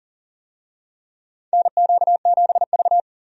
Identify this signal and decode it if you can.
Morse code, heard as a keyed tone, about 40 wpm, 700 Hz. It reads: NQ7V
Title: USA Call Signs - 4 Characters - 40wpm